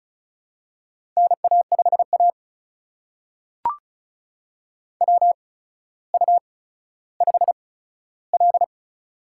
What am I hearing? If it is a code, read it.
Morse code, heard as a keyed tone, about 36 wpm, spaced as for 15 wpm, 700 Hz. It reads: NA5A WU5L
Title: USA Call Signs - 4 Characters - 8x Character Spacing - 35wpm